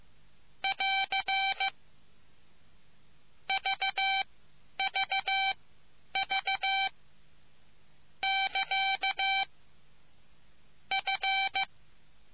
Morse code, heard as a keyed tone, about 15 wpm, spaced as for 10 wpm, 780 Hz. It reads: <AR> VVV <KA> F